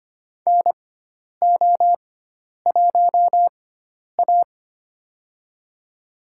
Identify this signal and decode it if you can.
Morse code, heard as a keyed tone, about 25 wpm, spaced as for 10 wpm, 700 Hz. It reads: NO1A
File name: Lesson 16g - Letter c - Call Signs 25wpm